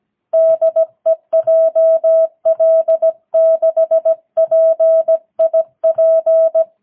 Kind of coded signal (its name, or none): Morse code